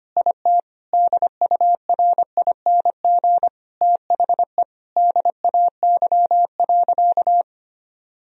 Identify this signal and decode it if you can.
Morse code, heard as a keyed tone, about 25 wpm, 700 Hz. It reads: IT DURING THE DAY.
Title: Sentences from Top 1000 Words - Review Part 2 - 25wpm